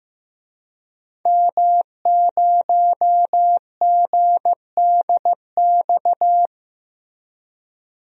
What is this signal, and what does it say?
Morse code, heard as a keyed tone, about 15 wpm, 700 Hz. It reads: M0GDX
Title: DXCC-100 Call Signs - 5 Characters - Repeated 2x - 15wpm